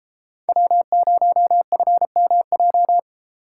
Morse code, heard as a keyed tone, about 31 wpm, 700 Hz. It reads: W0FMJ